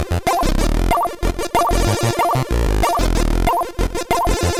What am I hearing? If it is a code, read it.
There is no code.